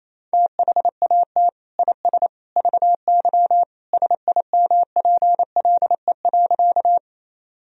Morse code, heard as a keyed tone, about 28 wpm, 700 Hz. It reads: THAT IS VY SIMPLE.
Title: Sentences from Top 400 Words and Abbreviations 28wpm